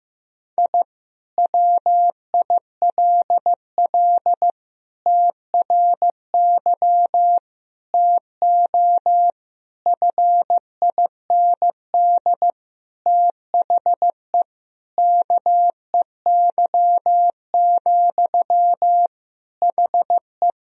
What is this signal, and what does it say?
Morse code, heard as a keyed tone, about 15 wpm, 700 Hz. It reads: I WILL TRY TO FIND THE KEY, HE
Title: Sentences from Top 1000 Words - Review Part 4 - 15wpm